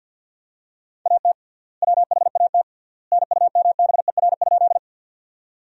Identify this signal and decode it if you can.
Morse code, heard as a keyed tone, about 50 wpm, 700 Hz. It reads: AT WHAT NUMBER?
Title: 3x Word-Spacing - Sentences from Top 100 Words - 50wpm